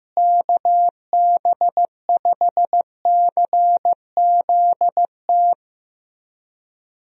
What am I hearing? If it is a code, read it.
Morse code, heard as a keyed tone, about 15 wpm, 700 Hz. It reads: KB5CZT
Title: USA Call Signs - 6 Characters - Repeated 3x - 15wpm